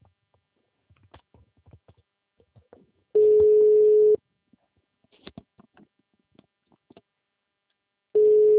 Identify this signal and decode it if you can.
Ringback tone (Europe, 425 Hz).